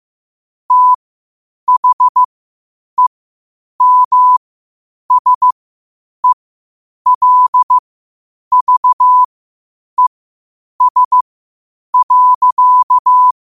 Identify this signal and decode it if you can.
Morse code, heard as a keyed tone, about 15 wpm, spaced as for 8 wpm, 1000 Hz. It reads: THEMSELVES.
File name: qotdfiles / 8wpm